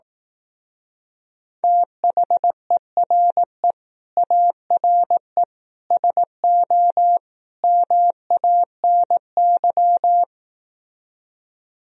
Morse code, heard as a keyed tone, about 18 wpm, 700 Hz. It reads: THERE ARE SO MANY
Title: Speed-Racing - Sets of 4 Words 15wpm